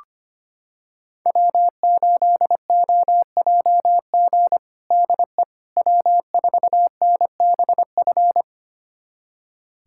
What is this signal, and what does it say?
Morse code, heard as a keyed tone, about 25 wpm, 700 Hz. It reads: W8OJG DE W4NBF
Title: USA Callsign DE Callsign - 5 Characters - 25wpm